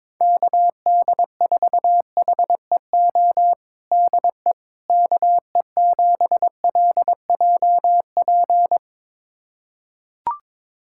Morse code, heard as a keyed tone, about 22 wpm, 700 Hz. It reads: KD4HEO DE KE7LJP
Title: USA Callsign DE Callsign 22wpm